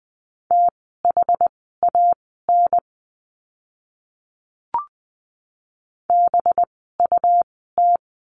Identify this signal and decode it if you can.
Morse code, heard as a keyed tone, about 20 wpm, spaced as for 15 wpm, 700 Hz. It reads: THAN BUT
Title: 185 Lesson 30f - Letter b - Words 20wpm